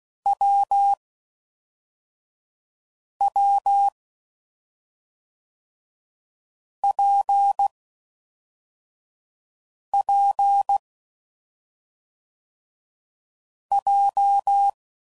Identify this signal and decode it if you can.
Morse code, heard as a keyed tone, about 16 wpm, 800 Hz. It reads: W W P P J
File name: Combinations / Difficult Combos / AWJ1